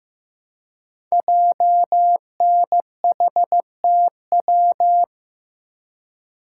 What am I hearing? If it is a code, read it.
Morse code, heard as a keyed tone, about 15 wpm, 700 Hz. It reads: JNHTW
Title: Code Groups - 5 Letters - 15wpm